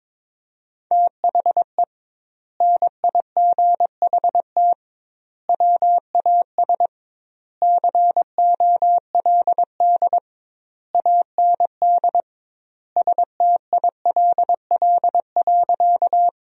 Morse code, heard as a keyed tone, about 22 wpm, 700 Hz. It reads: THE NIGHT WAS COLD AND STILL.
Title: 2x Word-Spacing - Sentences from Top 400 Words - 22wpm